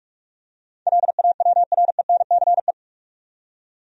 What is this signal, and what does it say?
Morse code, heard as a keyed tone, about 46 wpm, 700 Hz. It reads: LAWRENCE
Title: USA Common Names - 45wpm